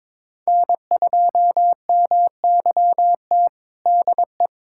Morse code, heard as a keyed tone, about 22 wpm, 700 Hz. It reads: N2MYT DE
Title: USA Callsign DE Callsign - 5 Characters - 22wpm